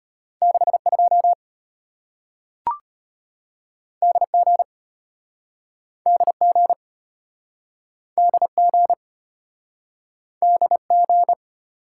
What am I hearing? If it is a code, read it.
Morse code, heard as a keyed tone, about 34 wpm, 700 Hz. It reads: 62 DG DG DG DG